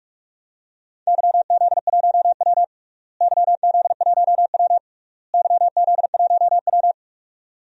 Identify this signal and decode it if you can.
Morse code, heard as a keyed tone, about 41 wpm, 690 Hz. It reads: YZ1W YZ1W YZ1W
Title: DXCC-100 Call Signs - 4 Characters - Repeated 3x - 45wpm